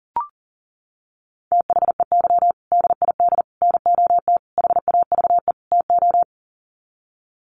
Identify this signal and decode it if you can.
Morse code, heard as a keyed tone, about 40 wpm, 700 Hz. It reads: THEY DID NOT HAVE TO
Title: Sets of 5 Words Encore2 40wpm